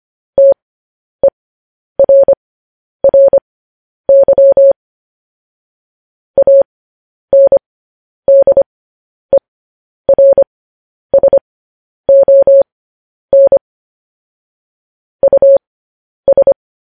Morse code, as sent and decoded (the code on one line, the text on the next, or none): - . .-. .-. -.-- / .- -. -.. .-. ... --- -. / ..- ...
TERRY ANDERSON US